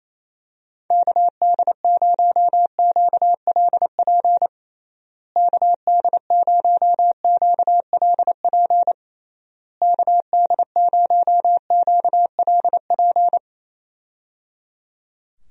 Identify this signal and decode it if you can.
Morse code, heard as a keyed tone, about 28 wpm, 700 Hz. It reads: KD0QLP KD0QLP KD0QLP
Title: USA Call Signs - 6 Characters - Repeated 3x - 28wpm